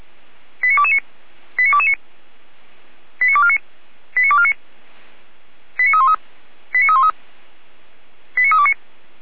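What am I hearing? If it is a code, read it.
Five-tone selective call (ZVEI), heard as quick runs of five tones, tones of about 70 ms. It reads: 89229 89229 89249 89249 89313 89313 89329
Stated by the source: ZVEI five-tone selective call received by SDR.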